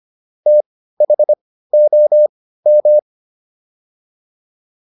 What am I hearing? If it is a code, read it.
Morse code, heard as a keyed tone, about 25 wpm, spaced as for 15 wpm, 600 Hz. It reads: THOM